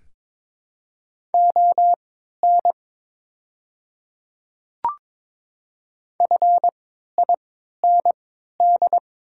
Morse code, heard as a keyed tone, about 22 wpm, spaced as for 12 wpm, 700 Hz. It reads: ON FIND
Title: 3x Character Spacing - Top 500 Words - 22wpm